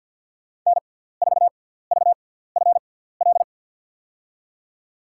Morse code, heard as a keyed tone, about 47 wpm, spaced as for 17 wpm, 700 Hz. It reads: N4VFL
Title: USA Call Signs - 5 Characters - 6x Character Spacing - 50wpm